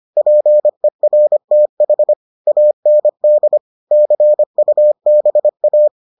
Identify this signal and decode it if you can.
Morse code, heard as a keyed tone, about 25 wpm, 600 Hz. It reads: PERTH AND CUBA